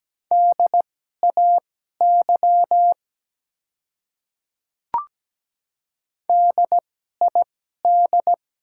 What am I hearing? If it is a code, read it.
Morse code, heard as a keyed tone, about 17 wpm, spaced as for 12 wpm, 700 Hz. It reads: DAY DID
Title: 2x Character Spacing - Top 100 Words - 17wpm